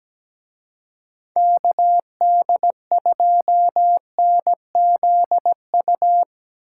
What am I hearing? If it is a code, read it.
Morse code, heard as a keyed tone, about 19 wpm, 700 Hz. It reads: KD2NZU
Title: Speed Racing - USA 2x3 Call Signs 17wpm